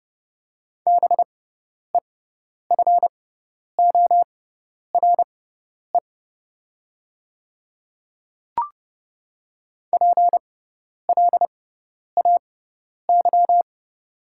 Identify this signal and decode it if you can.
Morse code, heard as a keyed tone, about 30 wpm, spaced as for 10 wpm, 700 Hz. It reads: BEFORE PLAY
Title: 6x Character Spacing - Top 200 Words - 30wpm